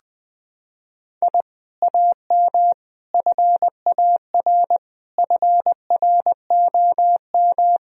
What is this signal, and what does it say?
Morse code, heard as a keyed tone, about 20 wpm, 700 Hz. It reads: I AM FAR FROM
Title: Sentences from Top 1000 Words - Review Part 3 - 20wpm